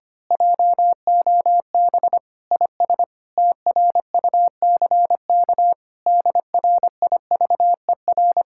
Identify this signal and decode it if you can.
Morse code, heard as a keyed tone, about 24 wpm, 700 Hz. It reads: JOB IS TRUCK DRIVER
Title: Speed-Racing Common QSO Phrases - 25wpm